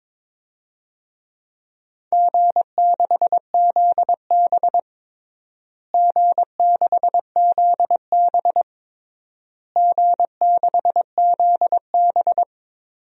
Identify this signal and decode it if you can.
Morse code, heard as a keyed tone, about 22 wpm, 700 Hz. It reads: G6ZB G6ZB G6ZB
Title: DXCC-100 Call Signs - 4 Characters - Repeated 3x - 22wpm